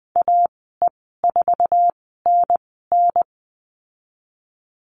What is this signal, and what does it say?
Morse code, heard as a keyed tone, about 20 wpm, spaced as for 15 wpm, 700 Hz. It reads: AE4NN
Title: 088 Lesson 15g - Number 5 - Call Signs 20wpm